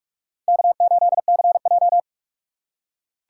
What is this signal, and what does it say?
Morse code, heard as a keyed tone, about 43 wpm, 700 Hz. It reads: K8CJ